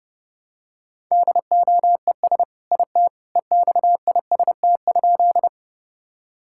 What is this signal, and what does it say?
Morse code, heard as a keyed tone, about 32 wpm, 700 Hz. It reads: DOES IT EXIST?
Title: Sentences from Top 1000 Words - Review Part 5 - 30wpm